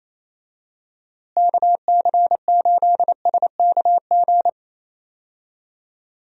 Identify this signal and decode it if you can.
Morse code, heard as a keyed tone, about 28 wpm, 700 Hz. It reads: KC8SKG
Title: USA Call Signs - 6 Characters - Repeated 2x - 28wpm